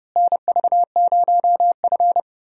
Morse code, heard as a keyed tone, about 30 wpm, 700 Hz. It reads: NV0F
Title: USA Call Signs - 4 Characters - Repeated 3x - 30wpm